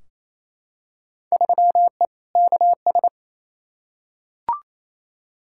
Morse code, heard as a keyed tone, about 28 wpm, 700 Hz. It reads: 3E KS
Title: Field Day Exchanges Encore - 28wpm